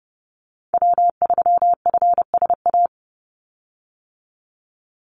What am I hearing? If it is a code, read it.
Morse code, heard as a keyed tone, about 29 wpm, 700 Hz. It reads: W3FSA
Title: Call Signs - 30wpm